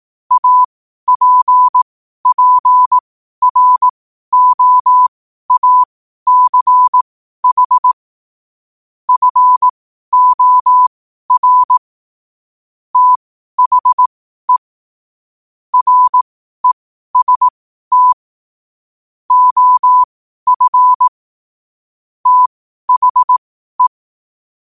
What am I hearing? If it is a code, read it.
Morse code, heard as a keyed tone, about 18 wpm, spaced as for 12 wpm, 1000 Hz. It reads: APPROACH FOR THE REST OF THE
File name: qotdfiles / 12wpm